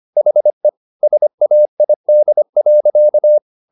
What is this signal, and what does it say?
Morse code, heard as a keyed tone, about 25 wpm, 600 Hz. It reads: HE SAID.